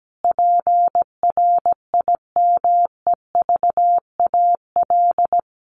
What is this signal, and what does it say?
Morse code, heard as a keyed tone, about 17 wpm, 700 Hz. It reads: PRIMEVAL
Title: A Sound Of Thunder 17wpm